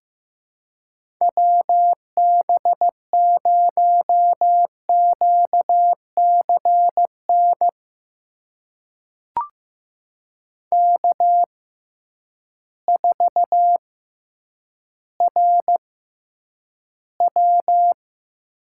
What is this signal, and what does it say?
Morse code, heard as a keyed tone, about 15 wpm, spaced as for 9 wpm, 700 Hz. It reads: WB0QCN K4RW
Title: USA Call Signs - 6x Character Spacing - 15wpm